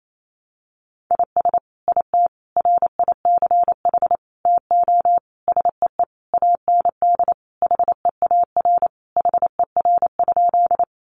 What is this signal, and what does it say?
Morse code, heard as a keyed tone, about 28 wpm, 700 Hz. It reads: IS IT RICH TO SEE AND HEAR HER?
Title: Sentences from Top 1100 Words 28wpm